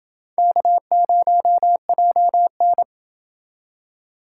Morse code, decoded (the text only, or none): K0JN